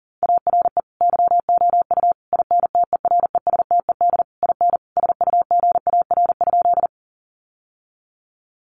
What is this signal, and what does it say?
Morse code, heard as a keyed tone, about 37 wpm, 700 Hz. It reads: ARE YOU INTERESTED IN SUGAR?
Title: Sentences from Top 1200 Words 40wpm